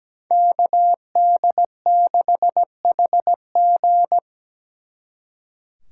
Morse code, heard as a keyed tone, about 20 wpm, 700 Hz. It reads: KD6HG
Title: Speed Racing - USA 2x2 Call Signs 17wpm